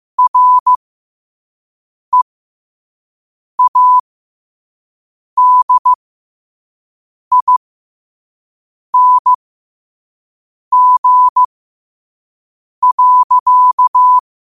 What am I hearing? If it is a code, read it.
Morse code, heard as a keyed tone, about 15 wpm, spaced as for 5 wpm, 1000 Hz. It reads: READING.
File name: qotdfiles / 5wpm